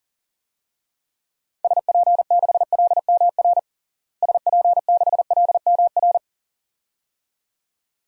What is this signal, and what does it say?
Morse code, heard as a keyed tone, about 40 wpm, 700 Hz. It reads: SP6LMR SP6LMR
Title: DXCC-100 Call Signs - 6 Characters - Repeated 2x - 40wpm